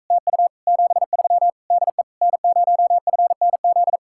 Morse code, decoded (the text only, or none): TU 73 DE N0FNZ